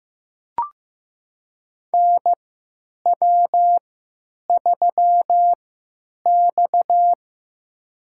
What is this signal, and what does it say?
Morse code, heard as a keyed tone, about 15 wpm, spaced as for 9 wpm, 700 Hz. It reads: NW3X
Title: USA Call Signs - 4 Characters - 3x Character Spacing - 15wpm